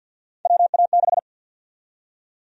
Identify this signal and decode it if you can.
Morse code, heard as a keyed tone, about 47 wpm, 700 Hz. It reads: WA6
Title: USA Call Sign Prefixes - Repeated 2x - 50wpm